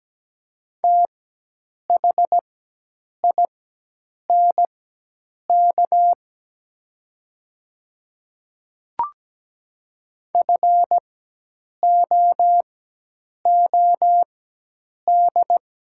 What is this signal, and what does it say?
Morse code, heard as a keyed tone, about 17 wpm, spaced as for 8 wpm, 700 Hz. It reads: THINK FOOD
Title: 4x Character Spacing - Top 300 Words - 17wpm